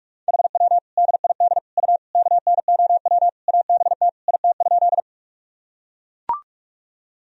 Morse code, heard as a keyed tone, about 43 wpm, 700 Hz. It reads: HW DID U KNOW ABT IT?
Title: Sentences from Top 100 Words and Abbreviations 45wpm